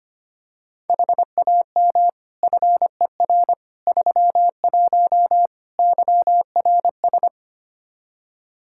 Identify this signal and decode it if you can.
Morse code, heard as a keyed tone, about 25 wpm, 700 Hz. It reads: HAM FER 31 YRS
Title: Practice QSO - Exchange 3 - More Info - 25wpm